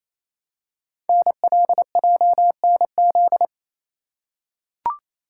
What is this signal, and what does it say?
Morse code, heard as a keyed tone, about 28 wpm, 700 Hz. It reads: NLJNZ
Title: Code Groups - 5 Letters - 28wpm